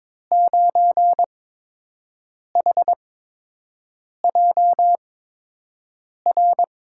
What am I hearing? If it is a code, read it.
Morse code, heard as a keyed tone, about 22 wpm, spaced as for 6 wpm, 700 Hz. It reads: 9HJR
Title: USA Call Signs - 6 Characters - 8x Character Spacing - 22wpm